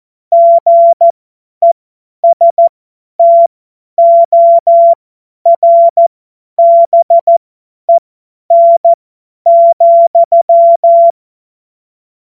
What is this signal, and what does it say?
Morse code, heard as a keyed tone, about 14 wpm, spaced as for 10 wpm, 680 Hz. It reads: GESTORBEN,
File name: Morsekurs / Witze / Witz 04. April 2023 50 F70